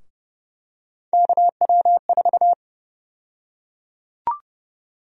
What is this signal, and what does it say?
Morse code, heard as a keyed tone, about 32 wpm, 700 Hz. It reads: KW4
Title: USA Call Sign Prefixes - 30wpm